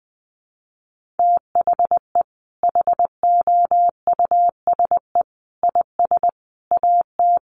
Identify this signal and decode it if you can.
Morse code, heard as a keyed tone, about 20 wpm, 700 Hz. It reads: THE HOUSE IS AT
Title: Sentences from Top 1500 Words 20wpm